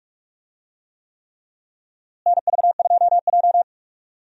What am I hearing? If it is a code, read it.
Morse code, heard as a keyed tone, about 42 wpm, 700 Hz. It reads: NV2J
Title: USA Call Signs - 4 Characters - Repeated 3x - 45wpm